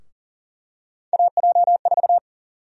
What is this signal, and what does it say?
Morse code, heard as a keyed tone, about 40 wpm, 700 Hz. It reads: AJ4
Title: USA Call Sign Prefixes - Repeated 2x - 40wpm